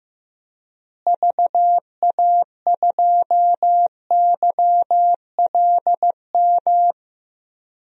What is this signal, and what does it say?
Morse code, heard as a keyed tone, about 15 wpm, 700 Hz. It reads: VA2YLM
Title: DXCC-100 Call Signs - 6 Characters - Repeated 2x - 15wpm